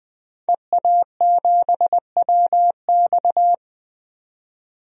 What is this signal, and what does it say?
Morse code, heard as a keyed tone, about 20 wpm, 700 Hz. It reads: EA7WX
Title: DXCC-100 Call Signs - 5 Characters - 20wpm